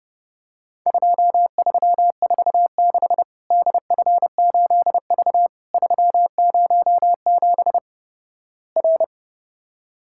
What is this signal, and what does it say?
Morse code, heard as a keyed tone, about 30 wpm, 700 Hz. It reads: 2346 DF8V 307 R
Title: Worked All Europe DX Contest - QTC Exchange - 30wpm